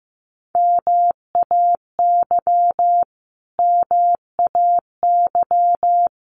Morse code, heard as a keyed tone, about 15 wpm, 700 Hz. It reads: MAY MAY
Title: Top 200 Words Repeated 15wpm